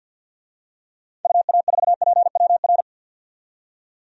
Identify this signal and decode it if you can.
Morse code, heard as a keyed tone, about 46 wpm, 700 Hz. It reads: UA4PWR
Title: DXCC-100 Call Signs - 6 Characters - 50wpm